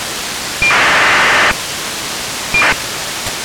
ACARS aircraft data.